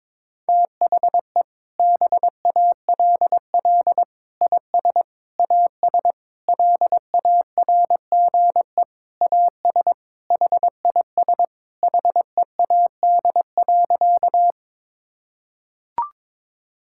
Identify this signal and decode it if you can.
Morse code, heard as a keyed tone, about 22 wpm, 700 Hz. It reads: THE BALL IS AS LARGE AS HIS HEAD.
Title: Sentences from Top 1000 Words - Review Part 6 - 22wpm